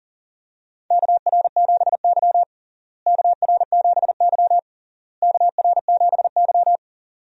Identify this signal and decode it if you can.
Morse code, heard as a keyed tone, about 37 wpm, 700 Hz. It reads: KR7Y KR7Y KR7Y
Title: USA Call Signs - 4 Characters - Repeated 3x - 40wpm